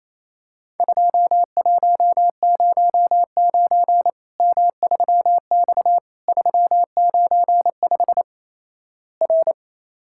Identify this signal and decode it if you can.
Morse code, heard as a keyed tone, about 28 wpm, 700 Hz. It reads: 2109 M3X 395 R
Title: Worked All Europe DX Contest - QTC Exchange - 28wpm